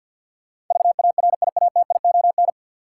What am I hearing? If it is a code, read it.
Morse code, heard as a keyed tone, about 48 wpm, 700 Hz. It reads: VARIATION
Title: Sentences from Top 1600 Words - New Word Drill - 50wpm